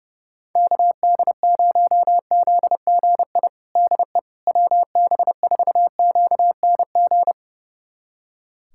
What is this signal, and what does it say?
Morse code, heard as a keyed tone, about 32 wpm, 700 Hz. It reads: KD0ZGI DE WB4QNG
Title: USA Callsign DE Callsign - 6 Characters - 30wpm